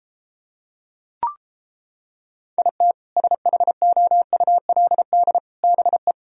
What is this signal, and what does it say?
Morse code, heard as a keyed tone, about 29 wpm, 700 Hz. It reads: IT SHOULD BE